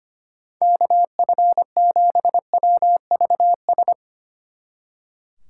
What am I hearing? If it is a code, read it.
Morse code, heard as a keyed tone, about 31 wpm, 700 Hz. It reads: KF7WVS
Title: Speed Racing - USA 2x3 Call Signs 25wpm